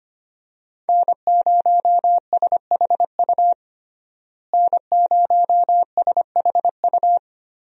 Morse code, heard as a keyed tone, about 25 wpm, 700 Hz. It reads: N0SHU N0SHU
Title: USA Call Signs - 5 Characters - Repeated 2x - 25wpm